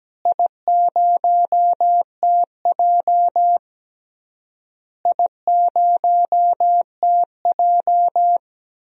Morse code, heard as a keyed tone, about 17 wpm, 700 Hz. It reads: I0TJ I0TJ
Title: DXCC-100 Call Signs - 4 Characters - Repeated 3x - 17wpm